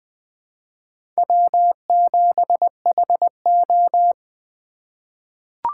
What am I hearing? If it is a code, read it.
Morse code, heard as a keyed tone, about 20 wpm, 700 Hz. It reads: W7HO
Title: 2023 SOTA Chaser Call Signs - 20wpm